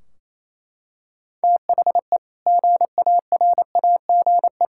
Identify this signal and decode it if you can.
Morse code, heard as a keyed tone, about 28 wpm, 700 Hz. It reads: THE GARAGE
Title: Sentences from Top 1600 Words 28wpm